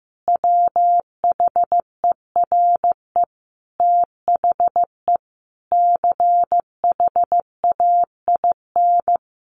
Morse code, heard as a keyed tone, about 15 wpm, 700 Hz. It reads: WHERE THE CHAIN
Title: Sentences from Top 1500 Words 15wpm